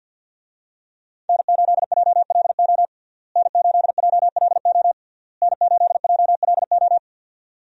Morse code, heard as a keyed tone, about 47 wpm, 700 Hz. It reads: N8JLO N8JLO N8JLO
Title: USA Call Signs - 5 Characters - Repeated 3x - 50wpm